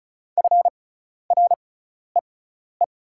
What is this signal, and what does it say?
Morse code, heard as a keyed tone, about 34 wpm, spaced as for 12 wpm, 690 Hz. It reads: FREE
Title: 6x Character Spacing - Top 500 Words - 35wpm